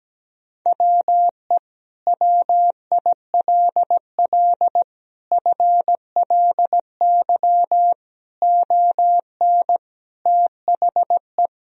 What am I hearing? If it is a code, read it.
Morse code, heard as a keyed tone, about 17 wpm, 700 Hz. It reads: WE WILL FLY ON THE
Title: Sentences from Top 1000 Words - Review Part 3 - 17wpm